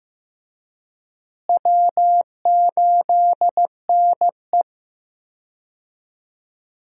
Morse code, heard as a keyed tone, about 16 wpm, 700 Hz. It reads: W8NE
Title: USA Call Signs - 4 Characters - Repeated 3x - 15wpm